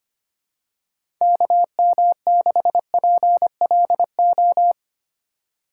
Morse code, heard as a keyed tone, about 25 wpm, 700 Hz. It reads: KM6PLO
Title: USA Call Signs - 6 Characters - 3x Character Spacing - 25wpm